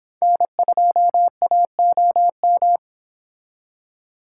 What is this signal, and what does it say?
Morse code, heard as a keyed tone, about 28 wpm, 700 Hz. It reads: N2AOM